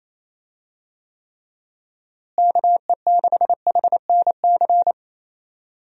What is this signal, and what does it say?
Morse code, heard as a keyed tone, about 27 wpm, 700 Hz. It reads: KE6HNC
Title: USA Call Signs - 6 Characters - Repeated 2x - 28wpm